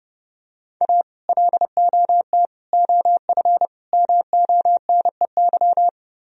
Morse code, heard as a keyed tone, about 30 wpm, 700 Hz. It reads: A LOT OF MONEY